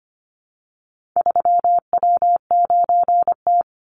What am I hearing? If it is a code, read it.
Morse code, heard as a keyed tone, about 25 wpm, 700 Hz. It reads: 3W9T
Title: Call Signs - 25wpm